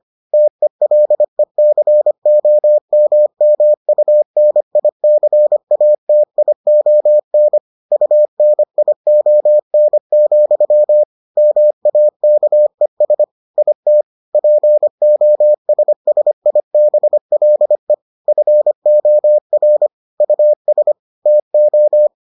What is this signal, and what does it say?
Morse code, heard as a keyed tone, about 25 wpm, 600 Hz. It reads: TELECOMMUNICATION UNION, MAKES IT POSSIBLE FOR US TO